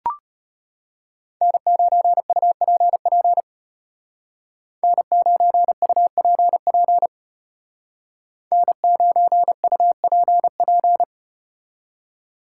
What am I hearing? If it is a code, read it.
Morse code, heard as a keyed tone, about 34 wpm, 700 Hz. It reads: N9UPP N9UPP N9UPP